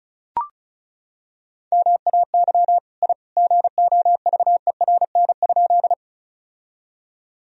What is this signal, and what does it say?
Morse code, heard as a keyed tone, about 35 wpm, 700 Hz. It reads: MAY I GOVERN?
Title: Sentences from Top 1000 Words - Review Part 2 - 35wpm